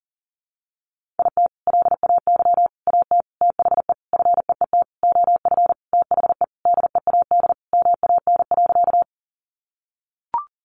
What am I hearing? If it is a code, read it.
Morse code, heard as a keyed tone, about 40 wpm, 700 Hz. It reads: IT LAY AT THE FEET OF THE DEAD MAN.
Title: Sentences from Top 1500 Words 40wpm